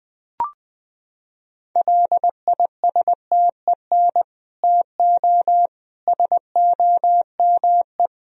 Morse code, heard as a keyed tone, about 20 wpm, 700 Hz. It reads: LISTEN TO SOME
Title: Sentences from Top 1000 Words - Review Part 1 - 20wpm